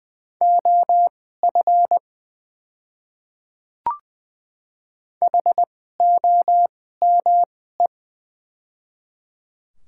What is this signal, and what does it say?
Morse code, heard as a keyed tone, about 20 wpm, spaced as for 14 wpm, 700 Hz. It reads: OF HOME
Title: 2x Character Spacing - Top 300 Words - 20wpm